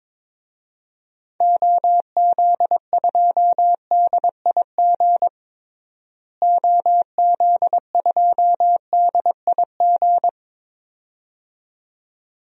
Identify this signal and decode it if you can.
Morse code, heard as a keyed tone, about 22 wpm, 700 Hz. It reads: OZ2DIG OZ2DIG
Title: DXCC-100 Call Signs - 6 Characters - Repeated 2x - 22wpm